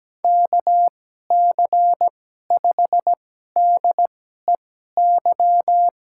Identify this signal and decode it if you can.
Morse code, heard as a keyed tone, about 17 wpm, spaced as for 12 wpm, 700 Hz. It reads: KC5DEY
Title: USA Call Signs - 6 Characters - 2x Character Spacing - 17wpm